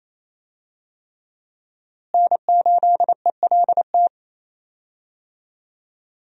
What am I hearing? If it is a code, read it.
Morse code, heard as a keyed tone, about 28 wpm, 700 Hz. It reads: N8ELT
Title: USA Call Signs - 5 Characters - Repeated 3x - 28wpm